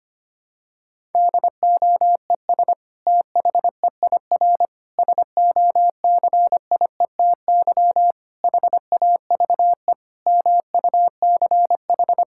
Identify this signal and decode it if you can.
Morse code, heard as a keyed tone, about 25 wpm, 700 Hz. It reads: DOES THEIR SOCIETY HAVE MUCH
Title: Sentences from Top 1000 Words - Review Part 6 - 25wpm